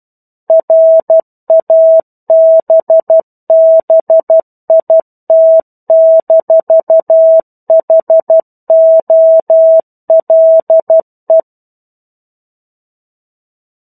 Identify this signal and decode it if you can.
Morse code, heard as a keyed tone, about 12 wpm, 650 Hz. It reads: RABBIT-HOLE
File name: 01-Chapter-01 12 WPM